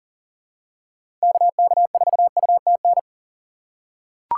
Morse code, heard as a keyed tone, about 40 wpm, 700 Hz. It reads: KK4UTN
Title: USA Call Signs - 2x Character Spacing - 40wpm